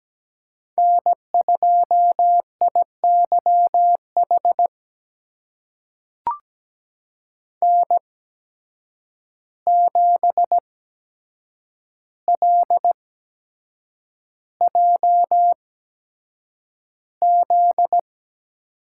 Morse code, heard as a keyed tone, about 17 wpm, spaced as for 7 wpm, 700 Hz. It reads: N2IYH N7LJZ
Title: USA Call Signs - 5 Characters - 8x Character Spacing - 17wpm